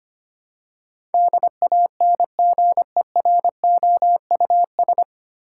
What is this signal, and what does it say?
Morse code, heard as a keyed tone, about 25 wpm, 700 Hz. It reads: DANGEROUS
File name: Nine Letter Words - 25wpm